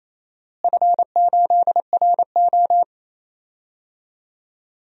Morse code, heard as a keyed tone, about 28 wpm, 700 Hz. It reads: F8RO
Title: DXCC-100 Call Signs - 4 Characters - Repeated 3x - 28wpm